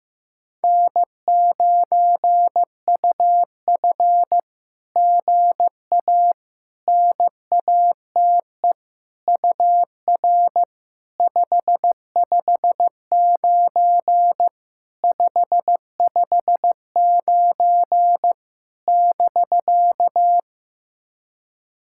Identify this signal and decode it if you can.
Morse code, heard as a keyed tone, about 15 wpm, 700 Hz. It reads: N9UF GA NATE UR 559 559 <BK>
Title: SOTA QSO Practice 15wpm